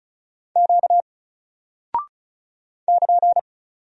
Morse code, heard as a keyed tone, about 36 wpm, 700 Hz. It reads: Q <KN>
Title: Single Character (Letter:Number:Punctuation:Prosign) 35wpm